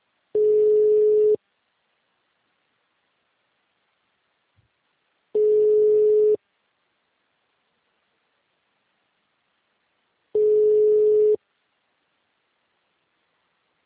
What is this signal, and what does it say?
Ringback tone (Europe, 425 Hz).
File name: out- -2005- - - .222